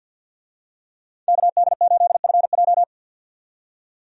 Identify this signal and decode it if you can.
Morse code, heard as a keyed tone, about 52 wpm, 700 Hz. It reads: KD8FJ
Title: USA Call Signs - 5 Characters - 50wpm